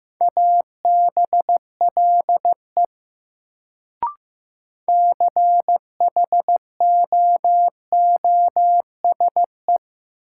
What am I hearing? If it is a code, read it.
Morse code, heard as a keyed tone, about 15 wpm, 700 Hz. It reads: ABLE CHOOSE
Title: Sentences from Top 700 Words - New Word Drill - 15wpm